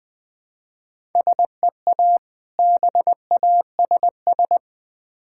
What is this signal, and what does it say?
Morse code, heard as a keyed tone, about 20 wpm, 700 Hz. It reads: SEA BASS
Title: Difficult Words with Hints - 20wpm